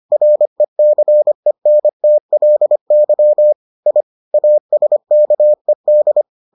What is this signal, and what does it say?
Morse code, heard as a keyed tone, about 25 wpm, 600 Hz. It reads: RECENTLY I ASKED